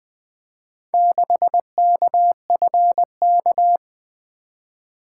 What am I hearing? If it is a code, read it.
Morse code, heard as a keyed tone, about 20 wpm, 700 Hz. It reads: 6KFK
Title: Four Letter-Number Combinations - 20wpm